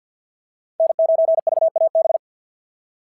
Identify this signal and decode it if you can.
Morse code, heard as a keyed tone, about 49 wpm, 690 Hz. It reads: N9VAB
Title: SOTA QSO Practice 50wpm